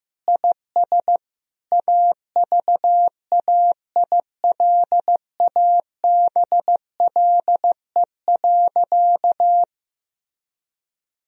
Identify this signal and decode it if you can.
Morse code, heard as a keyed tone, about 15 wpm, 700 Hz. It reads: IS AVAILABLE.
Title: Sentences from Top 1000 Words - Review Part 3 - 15wpm